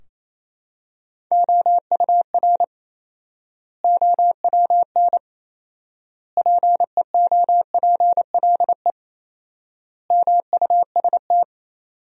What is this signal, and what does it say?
Morse code, heard as a keyed tone, about 28 wpm, 700 Hz. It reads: OUR OWN PEOPLE MUST
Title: 4x Word-Spacing - Sentences from Top 300 Words - 28wpm